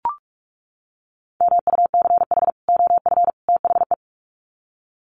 Morse code, heard as a keyed tone, about 44 wpm, 700 Hz. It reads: MUCH OF THE
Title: Sets of 3 Words 45wpm